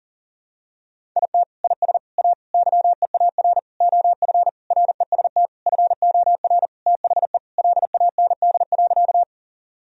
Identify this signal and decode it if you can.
Morse code, heard as a keyed tone, about 40 wpm, 700 Hz. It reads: IT IS A YEAR OF REST FOR THE LAND.
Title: Sentences from Top 1000 Words - Review Part 1 - 40wpm